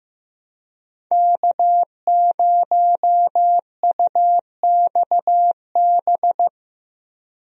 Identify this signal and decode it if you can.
Morse code, heard as a keyed tone, about 15 wpm, 700 Hz. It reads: K0UXB
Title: USA Call Signs - 5 Characters - Repeated 2x - 15wpm